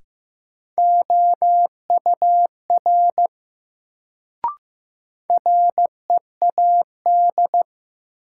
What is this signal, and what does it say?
Morse code, heard as a keyed tone, about 15 wpm, 700 Hz. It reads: OUR READ
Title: Sentences from Top 200 Words - New Word Drill - 15wpm